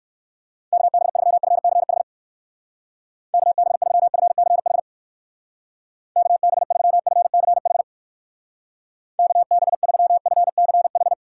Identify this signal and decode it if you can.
Morse code, heard as a keyed tone, about 60 wpm, 700 Hz. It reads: KB3FCH KB3FCH KB3FCH KB3FCH